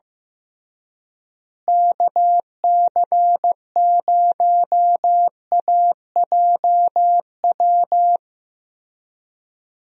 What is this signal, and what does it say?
Morse code, heard as a keyed tone, about 15 wpm, 700 Hz. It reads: KC0AJW
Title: USA Call Signs - 6 Characters - Repeated 3x - 15wpm